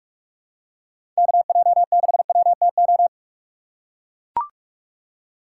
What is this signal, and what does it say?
Morse code, heard as a keyed tone, about 44 wpm, 700 Hz. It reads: KJ6WTO